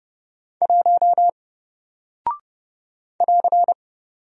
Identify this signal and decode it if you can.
Morse code, heard as a keyed tone, about 31 wpm, 700 Hz. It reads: 1 <AR>
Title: Single Character (Letter:Number:Punctuation:Prosign) Alt 30wpm